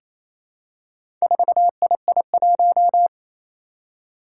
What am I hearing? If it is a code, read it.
Morse code, heard as a keyed tone, about 28 wpm, 700 Hz. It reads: 4II1
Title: Four Challenging Letter-Number Combinations 28wpm